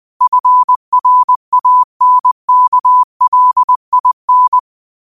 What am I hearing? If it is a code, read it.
Morse code, heard as a keyed tone, about 20 wpm, 1000 Hz. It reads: FRANKLIN
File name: qotdfiles / 20wpm